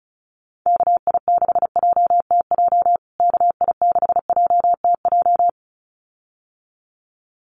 Morse code, heard as a keyed tone, about 34 wpm, 700 Hz. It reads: KI6JTJ KI6JTJ
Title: USA Call Signs Repeated 2x 35wpm